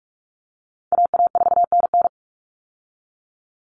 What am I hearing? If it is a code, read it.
Morse code, heard as a keyed tone, about 44 wpm, 700 Hz. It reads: AA4NN
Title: Call Signs Repeated 3x - 45wpm